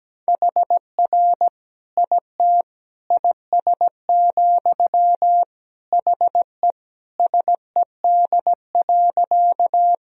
Morse code, heard as a keyed tone, about 17 wpm, 700 Hz. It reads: HR IT IS, HE SED.
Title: Sentences from Top 200 Words and Abbreviations 17wpm